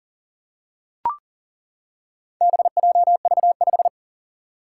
Morse code, heard as a keyed tone, about 40 wpm, 700 Hz. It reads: BJV5
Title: Four Challenging Letter-Number Combinations 40wpm